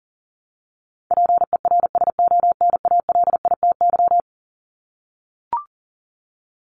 Morse code, heard as a keyed tone, about 40 wpm, 700 Hz. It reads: PERSONALITY
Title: Top 1251-1500 Words 40wpm